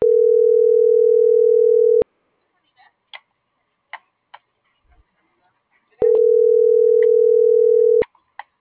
Ringback tone (North America), 440 + 480 Hz.
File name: CENTRAL GENERAL / out karla.sky